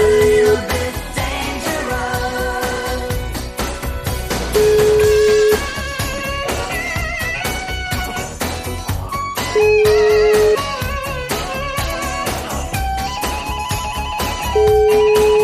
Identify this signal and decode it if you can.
Ringback tone (Europe, 425 Hz), with noise.